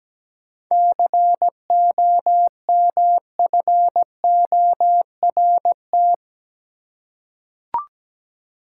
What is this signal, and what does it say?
Morse code, heard as a keyed tone, about 17 wpm, 700 Hz. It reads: COMFORT
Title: Seven Letter Words - 17wpm